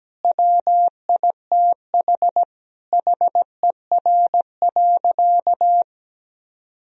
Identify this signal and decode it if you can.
Morse code, heard as a keyed tone, about 17 wpm, 700 Hz. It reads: WITH HER.
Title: Sentences from Top 1000 Words - Review Part 2 - 17wpm